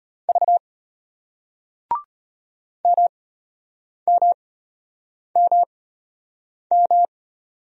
Morse code, heard as a keyed tone, about 36 wpm, spaced as for 8 wpm, 700 Hz. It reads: V MMMM